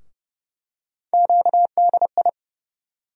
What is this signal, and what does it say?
Morse code, heard as a keyed tone, about 30 wpm, 700 Hz. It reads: QDI
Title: USA Call Sign Postfixes - 3 Letters 30wpm